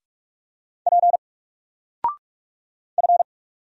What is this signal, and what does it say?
Morse code, heard as a keyed tone, about 44 wpm, 700 Hz. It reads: P F
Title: Single Character (Letter:Number:Punctuation:Prosign) Alt 45wpm